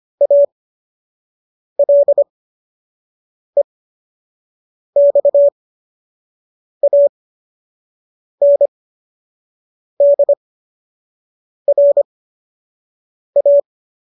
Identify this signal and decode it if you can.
Morse code, heard as a keyed tone, about 25 wpm, spaced as for 6 wpm, 570 Hz. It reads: ALEXANDRA
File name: qso012 06